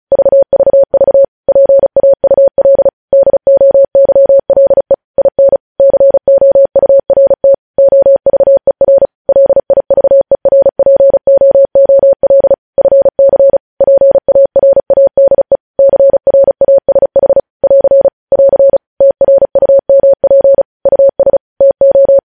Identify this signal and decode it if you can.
Morse code, heard as a keyed tone, about 36 wpm, 550 Hz. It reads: VVV PAUL DOYLE IN COURT OVER LIVERPOOL FC PARADE CRASH <AR> <AR> TRUMP US TO
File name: podcast / News headlines-1x-35x35